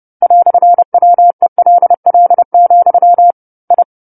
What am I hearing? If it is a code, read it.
Morse code, heard as a keyed tone, about 30 wpm, 700 Hz. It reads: "WELL, I